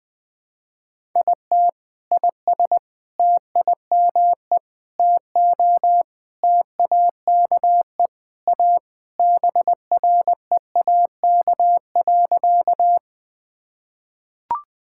Morse code, heard as a keyed tone, about 21 wpm, 690 Hz. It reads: IT IS TIME TO TAKE A BREAK.
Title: Sentences with 4-Word Phrases - Spoken Beginning - 20wpm